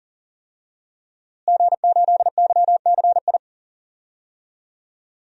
Morse code, heard as a keyed tone, about 40 wpm, 700 Hz. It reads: G8YCI
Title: DXCC-100 Call Signs - 5 Characters - Repeated 2x - 40wpm